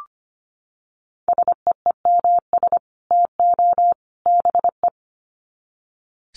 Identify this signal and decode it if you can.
Morse code, heard as a keyed tone, about 25 wpm, 700 Hz. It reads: SEEMS TO BE
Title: Sets of 3 Words 25wpm